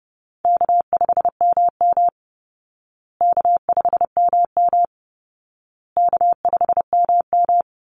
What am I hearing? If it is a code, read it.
Morse code, heard as a keyed tone, about 30 wpm, 700 Hz. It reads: K5MM K5MM K5MM
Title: Call Signs Repeated 2x - 30wpm